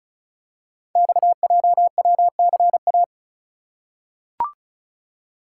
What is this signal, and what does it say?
Morse code, heard as a keyed tone, about 35 wpm, 700 Hz. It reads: XJWCA
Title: Code Groups - 5 Letter:Numbers - 35wpm